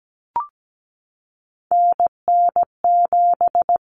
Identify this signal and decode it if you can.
Morse code, heard as a keyed tone, about 17 wpm, 700 Hz. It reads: NN7
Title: Common Cut Numbers - 3-Digit 17wpm